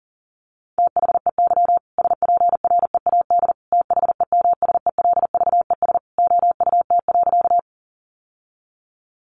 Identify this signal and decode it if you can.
Morse code, heard as a keyed tone, about 40 wpm, 700 Hz. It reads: THEY SPREAD THEMSELVES OUT.
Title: Sentences from Top 1200 Words 40wpm